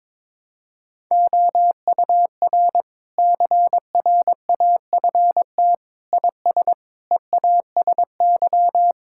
Morse code, heard as a keyed tone, about 22 wpm, 700 Hz. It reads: OUR CRAFT IS EASY
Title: Sentences from Top 1000 Words - Review Part 6 - 22wpm